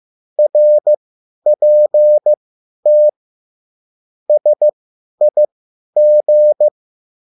Morse code, heard as a keyed tone, about 15 wpm, spaced as for 10 wpm, 600 Hz. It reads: RPT SIG
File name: QSL / CW lesson